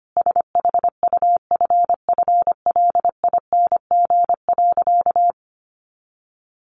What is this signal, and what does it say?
Morse code, heard as a keyed tone, about 25 wpm, 700 Hz. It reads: SHUFFLING.
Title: A Sound Of Thunder 25wpm